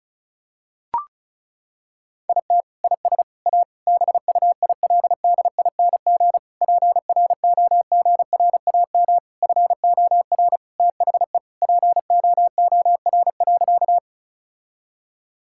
Morse code, heard as a keyed tone, about 36 wpm, 700 Hz. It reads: IT IS A BUILDING PROGRAM FOR THE POOR.
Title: Sentences from Top 1000 Words - Review Part 5 - 35wpm